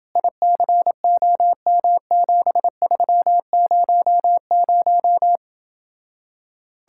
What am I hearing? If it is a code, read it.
Morse code, heard as a keyed tone, about 27 wpm, 700 Hz. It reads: ICOM7300